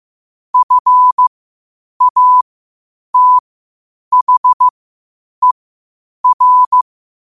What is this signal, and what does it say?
Morse code, heard as a keyed tone, about 15 wpm, spaced as for 8 wpm, 1000 Hz. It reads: FATHER
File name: qotdfiles / 8wpm